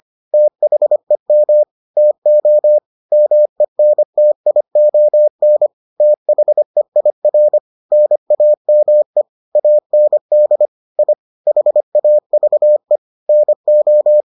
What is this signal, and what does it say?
Morse code, heard as a keyed tone, about 25 wpm, 600 Hz. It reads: THEM TO MENTION THEIR NAME AND I HAVE NO